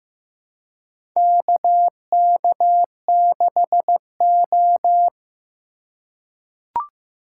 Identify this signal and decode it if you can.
Morse code, heard as a keyed tone, about 15 wpm, 700 Hz. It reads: KK6O
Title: USA Call Signs - 4 Characters - 2x Character Spacing - 15wpm